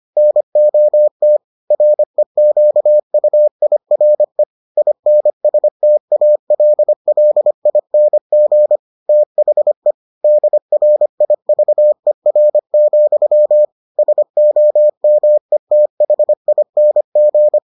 Morse code, heard as a keyed tone, about 25 wpm, 600 Hz. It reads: NOT REQUIRE INSTALLING THE DRIVER, SOMETHING